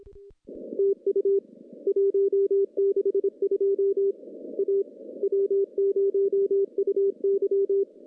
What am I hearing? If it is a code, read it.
Morse code, heard as a keyed tone, about 26 wpm, 400 Hz. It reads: TU 162 A W0UY
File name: contests / CW-SS-k5tr- -1400